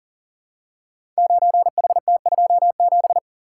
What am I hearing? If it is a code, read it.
Morse code, heard as a keyed tone, about 41 wpm, 700 Hz. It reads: 9HT27
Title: Code Groups - 5 Letter:Numbers - 40wpm